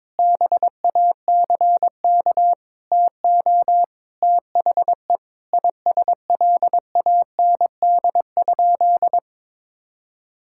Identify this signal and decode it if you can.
Morse code, heard as a keyed tone, about 21 wpm, 700 Hz. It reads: BACK TO THE ISLAND?
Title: Sentences from Top 1600 Words 22wpm